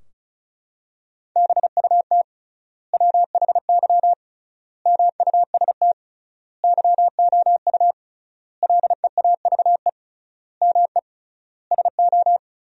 Morse code, heard as a keyed tone, about 36 wpm, 700 Hz. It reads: BUT WHY MUST YOU LEAVE ME SO
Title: 3x Word-Spacing - Sentences from Top 400 Words - 35wpm